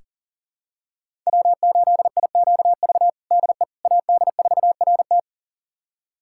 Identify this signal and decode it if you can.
Morse code, heard as a keyed tone, about 40 wpm, 700 Hz. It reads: W8IQV DE AD4RT
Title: USA Callsign DE Callsign 40wpm